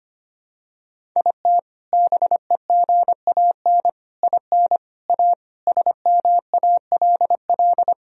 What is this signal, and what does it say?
Morse code, heard as a keyed tone, about 25 wpm, 700 Hz. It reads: IT BEGAN IN A SMALL
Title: Sentences from Top 1000 Words - Review Part 1 - 25wpm